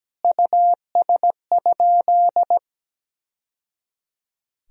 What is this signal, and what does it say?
Morse code, heard as a keyed tone, about 17 wpm, 700 Hz. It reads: US?
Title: Sentences from Top 1000 Words - Review Part 3 - 17wpm